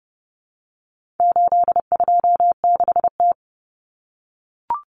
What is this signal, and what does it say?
Morse code, heard as a keyed tone, about 30 wpm, 700 Hz. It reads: 826T
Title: Common Cut Numbers - 4-Digit 30wpm